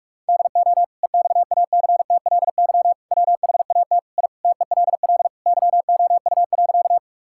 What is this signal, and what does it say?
Morse code, heard as a keyed tone, about 41 wpm, 700 Hz. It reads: DO EXACTLY WHAT I TELL YOU.
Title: Sentences from Top 1000 Words - Review Part 6 - 45wpm